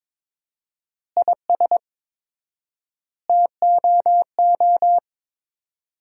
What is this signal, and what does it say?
Morse code, heard as a keyed tone, about 22 wpm, 700 Hz. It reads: IS TOO
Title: 4x Word-Spacing - Sentences from Top 400 Words - 22wpm